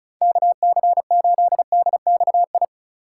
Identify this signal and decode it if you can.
Morse code, heard as a keyed tone, about 37 wpm, 700 Hz. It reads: KC8DXI
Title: Speed Racing - USA 2x3 Call Signs 30wpm